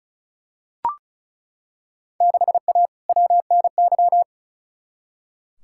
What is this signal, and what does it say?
Morse code, heard as a keyed tone, about 36 wpm, 700 Hz. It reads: 6A WNY
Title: Field Day Exchanges Encore - 35wpm